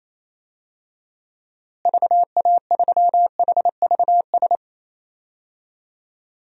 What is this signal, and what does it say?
Morse code, heard as a keyed tone, about 28 wpm, 700 Hz. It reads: VA3HVS
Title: DXCC-100 Call Signs - 6 Characters - Repeated 2x - 28wpm